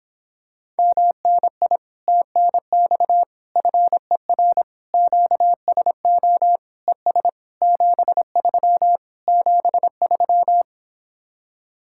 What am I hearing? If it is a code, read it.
Morse code, heard as a keyed tone, about 26 wpm, 700 Hz. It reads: MNI TNX FER QSO ES 73 73